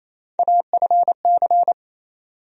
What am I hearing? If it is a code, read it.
Morse code, heard as a keyed tone, about 28 wpm, 700 Hz. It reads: AFC
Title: USA Call Sign Postfixes - 3 Letters 28wpm